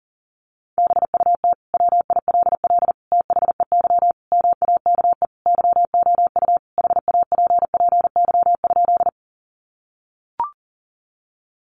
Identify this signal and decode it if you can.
Morse code, heard as a keyed tone, about 40 wpm, 700 Hz. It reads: BUT WILL THEY MAKE YOU HAPPY?
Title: Sentences from Top 1000 Words 40wpm